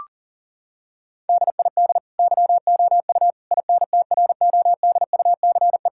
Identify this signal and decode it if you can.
Morse code, heard as a keyed tone, about 40 wpm, 700 Hz. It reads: DID YOU INTRODUCE
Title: Sentences from Top 1600 Words 40wpm